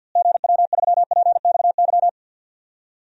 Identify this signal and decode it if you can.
Morse code, heard as a keyed tone, about 51 wpm, 700 Hz. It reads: GW3PXY
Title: DXCC-100 Call Signs - 6 Characters - Repeated 3x - 50wpm